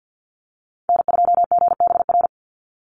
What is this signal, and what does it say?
Morse code, heard as a keyed tone, about 49 wpm, 700 Hz. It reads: N2GBR
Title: NA SOTA Chaser Callsigns 50wpm